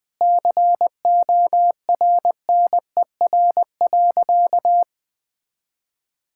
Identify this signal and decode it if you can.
Morse code, heard as a keyed tone, about 20 wpm, 700 Hz. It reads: CORNER.
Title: Sentences from Top 1600 Words 20wpm